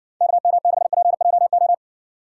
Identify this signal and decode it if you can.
Morse code, heard as a keyed tone, about 60 wpm, 700 Hz. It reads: KM6PJO